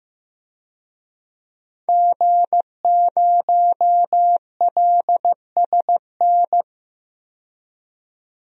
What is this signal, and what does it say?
Morse code, heard as a keyed tone, about 15 wpm, 700 Hz. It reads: G0LSN
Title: DXCC-100 Call Signs - 5 Characters - 15wpm